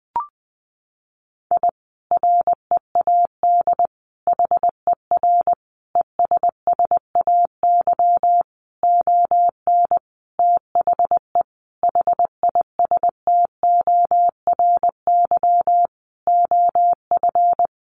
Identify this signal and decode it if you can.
Morse code, heard as a keyed tone, about 20 wpm, 700 Hz. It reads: I READ HER ESSAY ON THE HISTORY OF
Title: Sentences from Top 1500 Words 20wpm